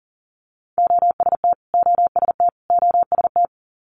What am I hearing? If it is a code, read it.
Morse code, heard as a keyed tone, about 42 wpm, 690 Hz. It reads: OST OST OST
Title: Multi-letter Phonemes Repeated 3x 40wpm